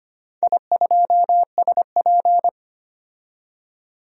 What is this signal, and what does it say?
Morse code, heard as a keyed tone, about 25 wpm, 700 Hz. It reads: I2SP
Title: DXCC-100 Call Signs - 4 Characters - 25wpm